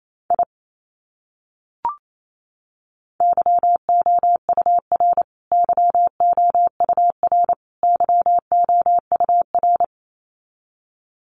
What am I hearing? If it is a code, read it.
Morse code, heard as a keyed tone, about 28 wpm, 700 Hz. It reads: I YOUR YOUR YOUR
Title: Top 200 Words Repeated 28wpm